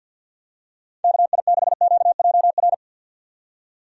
Morse code, heard as a keyed tone, about 45 wpm, 700 Hz. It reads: KI6QJR
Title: USA Call Signs - 6 Characters - 50wpm